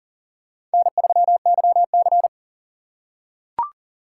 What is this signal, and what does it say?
Morse code, heard as a keyed tone, about 40 wpm, 700 Hz. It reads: N3YC
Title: USA Call Signs - 4x Character Spacing - 40wpm